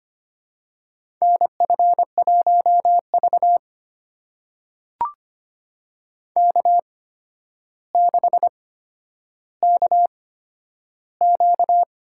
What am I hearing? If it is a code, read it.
Morse code, heard as a keyed tone, about 25 wpm, spaced as for 11 wpm, 700 Hz. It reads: NF1V K6KQ
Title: USA Call Signs - 4 Characters - 8x Character Spacing - 25wpm